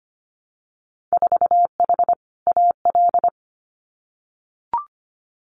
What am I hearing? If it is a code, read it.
Morse code, heard as a keyed tone, about 25 wpm, 700 Hz. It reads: 4H AL
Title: Winter Field Day Exchanges Encore 25wpm